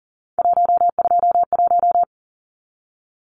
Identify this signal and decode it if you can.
Morse code, heard as a keyed tone, about 40 wpm, 700 Hz. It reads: 121
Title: Numbers - 3-Digit 40wpm